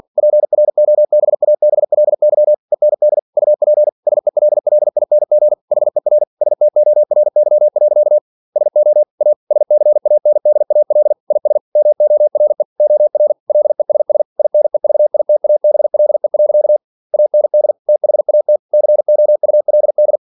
Morse code, heard as a keyed tone, about 46 wpm, 600 Hz. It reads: PROBABLY END UP SELLING HER STORY. SO A SCANDAL IS MORE OR LESS INEVITABLE. AND THAT COULD
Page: Never 47wpm